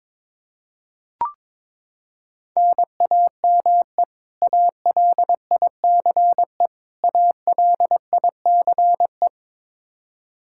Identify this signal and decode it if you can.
Morse code, heard as a keyed tone, about 22 wpm, 700 Hz. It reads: NAME ALICE ALICE
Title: Practice QSO - Exchange 1 - Beginning - 22wpm